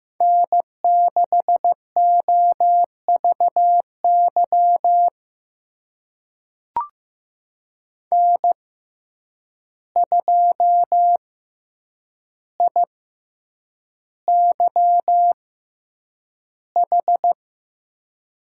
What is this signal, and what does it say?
Morse code, heard as a keyed tone, about 15 wpm, spaced as for 8 wpm, 700 Hz. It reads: N6OVY N2IYH
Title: USA Call Signs - 5 Characters - 6x Character Spacing - 15wpm